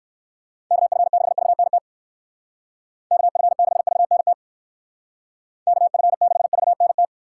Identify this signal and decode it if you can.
Morse code, heard as a keyed tone, about 59 wpm, 700 Hz. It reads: KF6VNT KF6VNT KF6VNT